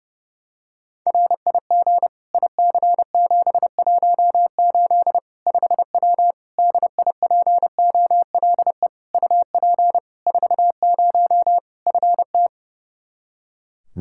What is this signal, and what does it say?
Morse code, heard as a keyed tone, about 30 wpm, 700 Hz. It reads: RIG IC718 5W DIPOLE UP 40 FT
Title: K1USN SST QSO Practice 30wpm